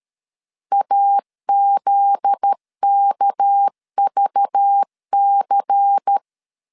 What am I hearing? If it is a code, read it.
Morse code, heard as a keyed tone, about 13 wpm, 800 Hz. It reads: AZKVC
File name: CW / DET / 1 9 1500C S C 12wpm 3811